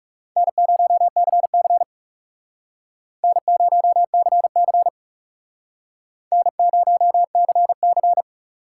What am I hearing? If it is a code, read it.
Morse code, heard as a keyed tone, about 39 wpm, 700 Hz. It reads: N0CC N0CC N0CC